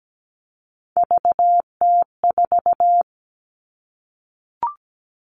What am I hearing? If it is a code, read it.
Morse code, heard as a keyed tone, about 17 wpm, 700 Hz. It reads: VT4
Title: All Cut Numbers - 3-Digit 17wpm